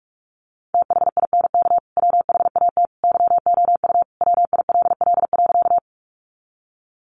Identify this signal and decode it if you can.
Morse code, heard as a keyed tone, about 46 wpm, 700 Hz. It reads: THINK WHAT YOU WILL.
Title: Sentences from Top 200 Words 45wpm